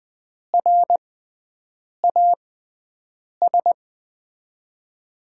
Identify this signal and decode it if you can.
Morse code, heard as a keyed tone, about 20 wpm, spaced as for 7 wpm, 700 Hz. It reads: RAS
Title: USA Call Signs - 5 Characters - 6x Character Spacing - 20wpm